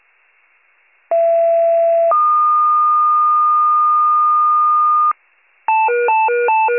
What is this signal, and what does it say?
Two-tone page, heard as a one-second tone, then a three-second tone, about 670 then 1220 Hz. It is heard from 1.1 to 5.1 s.